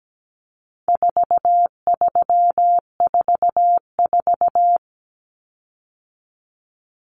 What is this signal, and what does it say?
Morse code, heard as a keyed tone, about 17 wpm, 700 Hz. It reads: 4344
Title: Common Cut Numbers - 4-Digit 17wpm